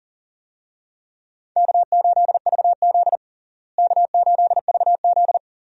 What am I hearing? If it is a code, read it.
Morse code, heard as a keyed tone, about 40 wpm, 700 Hz. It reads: K8VZ K8VZ
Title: USA Call Signs - 4 Characters - Repeated 3x - 40wpm